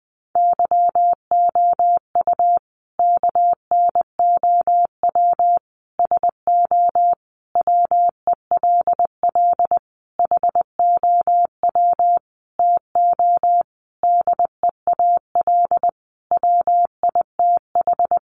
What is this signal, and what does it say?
Morse code, heard as a keyed tone, about 20 wpm, 700 Hz. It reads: YOU KNOW SO WELL HOW TO DEAL WITH
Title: Sentences from Top 800 Words 20wpm